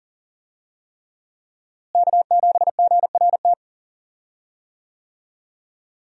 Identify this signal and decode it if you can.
Morse code, heard as a keyed tone, about 37 wpm, 700 Hz. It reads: K7GRT
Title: USA Call Signs - 5 Characters - Repeated 3x - 40wpm